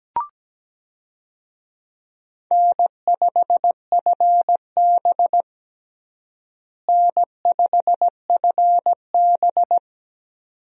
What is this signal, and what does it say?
Morse code, heard as a keyed tone, about 17 wpm, 700 Hz. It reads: N5FB N5FB
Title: USA Call Signs - 4 Characters - Repeated 2x - 17wpm